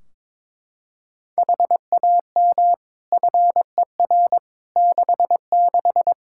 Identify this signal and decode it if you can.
Morse code, heard as a keyed tone, about 22 wpm, 700 Hz. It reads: HAM FER 66
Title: K1USN SST QSO Practice 22wpm